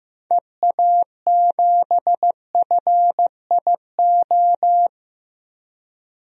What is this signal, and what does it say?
Morse code, heard as a keyed tone, about 15 wpm, 700 Hz. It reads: EA7FIO
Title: DXCC-100 Call Signs - 6 Characters - Repeated 2x - 15wpm